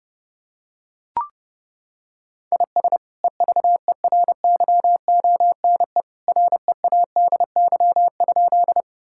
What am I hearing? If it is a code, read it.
Morse code, heard as a keyed tone, about 32 wpm, 700 Hz. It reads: IS EVERYONE READY?
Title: Sentences from Top 1000 Words - Review Part 5 - 30wpm